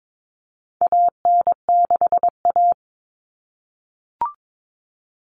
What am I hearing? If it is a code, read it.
Morse code, heard as a keyed tone, about 22 wpm, 700 Hz. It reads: AN6A
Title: Common Cut Numbers - 4-Digit 22wpm